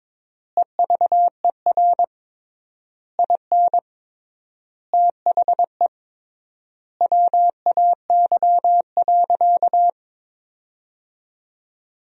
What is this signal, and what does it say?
Morse code, heard as a keyed tone, about 22 wpm, 700 Hz. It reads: EVER IN THE WAY.
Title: 3x Word-Spacing - Sentences from Top 400 Words - 22wpm